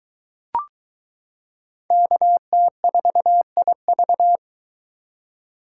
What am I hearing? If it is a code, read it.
Morse code, heard as a keyed tone, about 20 wpm, 700 Hz. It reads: KT4IV